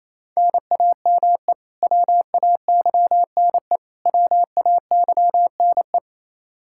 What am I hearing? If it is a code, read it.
Morse code, heard as a keyed tone, about 28 wpm, 700 Hz. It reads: NAME WAYNE WAYNE
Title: Practice QSO - Exchange 1 - Beginning - 28wpm